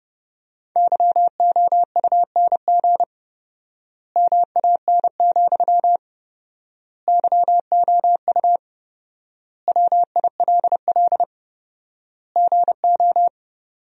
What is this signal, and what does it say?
Morse code, heard as a keyed tone, about 30 wpm, 700 Hz. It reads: YOUNG MAN, YOU WILL GO
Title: 4x Word-Spacing - Sentences from Top 400 Words - 30wpm